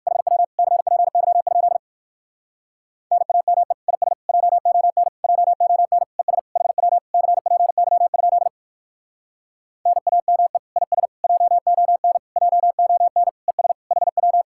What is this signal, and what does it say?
Morse code, heard as a keyed tone, about 59 wpm, 700 Hz. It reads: HW CPY? NAME IS JON JON ES HW CPY? NAME IS JON JON ES HW